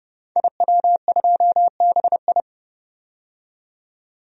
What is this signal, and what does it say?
Morse code, heard as a keyed tone, about 30 wpm, 700 Hz. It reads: IW2BI
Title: DXCC-100 Call Signs - 5 Characters - 30wpm